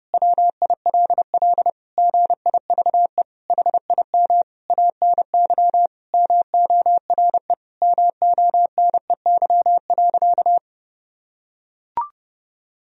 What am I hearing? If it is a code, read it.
Morse code, heard as a keyed tone, about 30 wpm, 700 Hz. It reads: WILL GIVE HIM ANY MORE MONEY.
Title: Sentences from Top 1000 Words - Review Part 4 - 30wpm